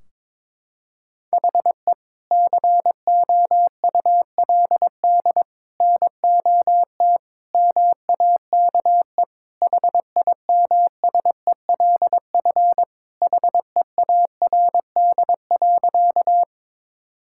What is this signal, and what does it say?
Morse code, heard as a keyed tone, about 22 wpm, 700 Hz. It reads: HE COULD NOT MAKE HIMSELF HEARD.
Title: Sentences from Top 1600 Words 22wpm